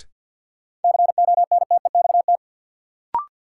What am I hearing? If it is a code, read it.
Morse code, heard as a keyed tone, about 53 wpm, 700 Hz. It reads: CONTEXT
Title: Sentences from Top 1000 Words - Review Part 7 - New Word Drill - 50wpm